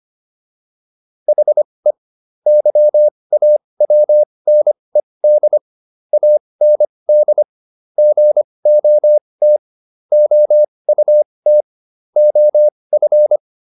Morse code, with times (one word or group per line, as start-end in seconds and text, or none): 1.3-1.9 HE
2.5-5.6 YAWNED
6.1-7.4 AND
8.0-9.6 GOT
10.1-11.6 OUT
12.2-13.4 OF